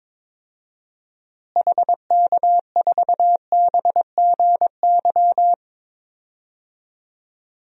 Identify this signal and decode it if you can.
Morse code, heard as a keyed tone, about 22 wpm, 700 Hz. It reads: HK4BGY
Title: DXCC-100 Call Signs - 6 Characters - Repeated 3x - 22wpm